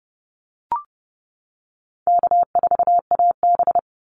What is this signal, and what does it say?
Morse code, heard as a keyed tone, about 30 wpm, 700 Hz. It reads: K4AB
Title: Easy Call Signs - 30wpm